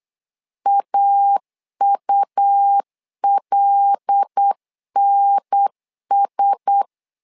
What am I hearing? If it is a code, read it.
Morse code, heard as a keyed tone, about 8 wpm, 800 Hz. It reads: AULNS
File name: LEZIONI / 5 LBVC / 5 L 8wpm 2831